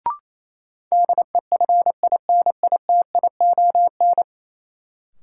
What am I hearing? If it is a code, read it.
Morse code, heard as a keyed tone, about 28 wpm, 700 Hz. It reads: DEFINITION
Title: Sentences from Top 1000 Words - Review Part 4 - New Word Drill - 28wpm